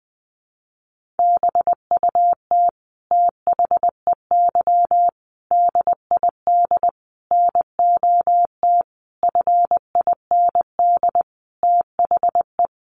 Morse code, heard as a keyed tone, about 20 wpm, 700 Hz. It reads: BUT THEY DID NOT FIND THE
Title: Sentences from Top 1400 Words 20wpm